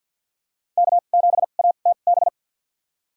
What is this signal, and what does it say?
Morse code, heard as a keyed tone, about 47 wpm, spaced as for 35 wpm, 700 Hz. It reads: K7ATB
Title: USA Call Signs - 2x Character Spacing - 50wpm